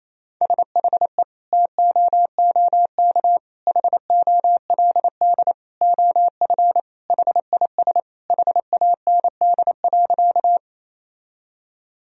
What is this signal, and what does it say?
Morse code, heard as a keyed tone, about 28 wpm, 700 Hz. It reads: SHE TOOK HOLD OF HIS HAND.
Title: Sentences from Top 1000 Words - Review Part 2 - 28wpm